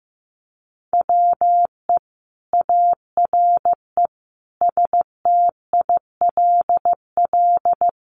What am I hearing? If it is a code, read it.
Morse code, heard as a keyed tone, about 15 wpm, 700 Hz. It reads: WE ARE STILL
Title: Sentences from Top 1300 Words 15wpm